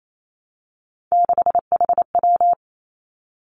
Morse code, heard as a keyed tone, about 28 wpm, 700 Hz. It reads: 6HW
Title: Three Challenging Letter-Number Combinations 28wpm